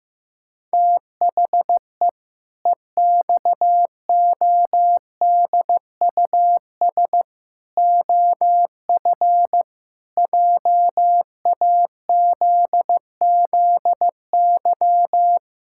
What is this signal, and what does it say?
Morse code, heard as a keyed tone, about 15 wpm, 700 Hz. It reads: THE EXODUS OF JAZZY
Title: Pangram Challenge v2 - 15wpm